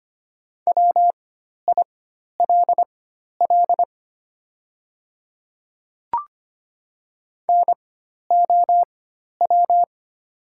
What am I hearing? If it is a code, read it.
Morse code, heard as a keyed tone, about 25 wpm, spaced as for 12 wpm, 700 Hz. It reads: WILL NOW
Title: 4x Character Spacing - Top 100 Words - 25wpm